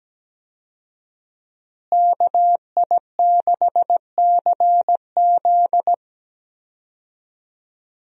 Morse code, heard as a keyed tone, about 17 wpm, 700 Hz. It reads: KI6CZ
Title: USA Call Signs - 5 Characters - Repeated 2x - 17wpm